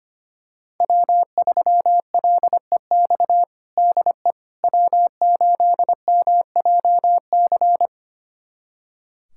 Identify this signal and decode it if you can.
Morse code, heard as a keyed tone, about 25 wpm, 700 Hz. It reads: W3LEX DE W8MJC
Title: USA Callsign DE Callsign - 5 Characters - 25wpm